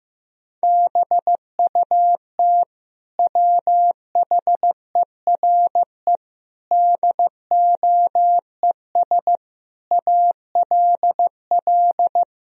Morse code, heard as a keyed tone, about 16 wpm, 700 Hz. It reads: BUT WHERE DOES ALL
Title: Sentences from Top 1000 Words - Review Part 2 - 15wpm